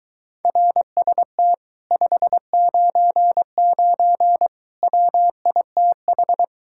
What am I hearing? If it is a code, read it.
Morse code, heard as a keyed tone, about 22 wpm, 700 Hz. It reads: RST 599 WITH